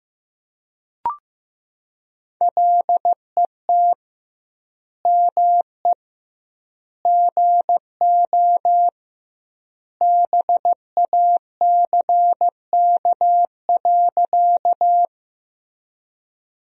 Morse code, heard as a keyed tone, about 15 wpm, 700 Hz. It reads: LET ME GO BACK.
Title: 2x Word-Spacing - Sentences from Top 300 Words - 15wpm